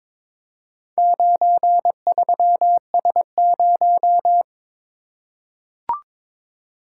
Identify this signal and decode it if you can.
Morse code, heard as a keyed tone, about 22 wpm, 700 Hz. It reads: 93S0
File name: practice / 22wpm / Four Letter-Number Combinations - 22wpm